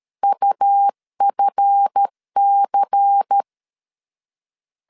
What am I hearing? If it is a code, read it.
Morse code, heard as a keyed tone, about 13 wpm, 800 Hz. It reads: UFC
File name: LEZIONI / 7 GF9Z / 7 GF 12wpm 2122